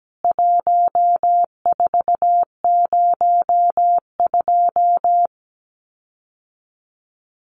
Morse code, heard as a keyed tone, about 17 wpm, 700 Hz. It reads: 1402
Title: CWT Exchange 17wpm